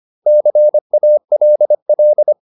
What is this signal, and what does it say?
Morse code, heard as a keyed tone, about 25 wpm, 600 Hz. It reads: CALL
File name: SS319 25